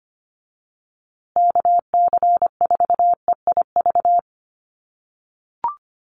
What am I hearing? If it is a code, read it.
Morse code, heard as a keyed tone, about 25 wpm, 700 Hz. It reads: KC4EIV
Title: USA Call Signs Repeated 3x 25wpm